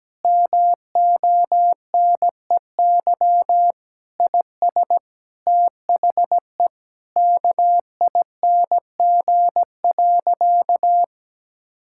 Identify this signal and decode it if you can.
Morse code, heard as a keyed tone, about 17 wpm, 700 Hz. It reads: MONEY IS THE KING.
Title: Sentences from Top 1600 Words 17wpm